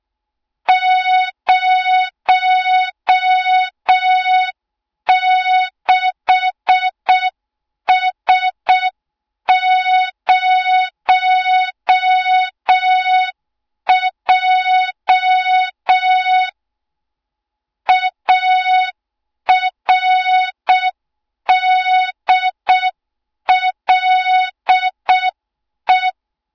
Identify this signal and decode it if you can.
Morse code, heard as a keyed tone, about 6 wpm, 3700 Hz. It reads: 06S0J ARDLE
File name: Morze-HA2MN / hang / 20 GYAKORLAT